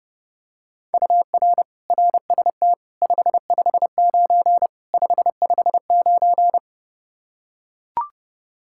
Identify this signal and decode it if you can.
Morse code, heard as a keyed tone, about 30 wpm, 700 Hz. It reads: UR RST 559 559
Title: Practice QSO - Exchange 1 - Beginning - 30wpm